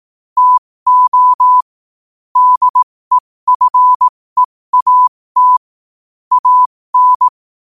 Morse code, heard as a keyed tone, about 18 wpm, spaced as for 15 wpm, 1000 Hz. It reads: TO DEFEAT AN
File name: qotdfiles / 15wpm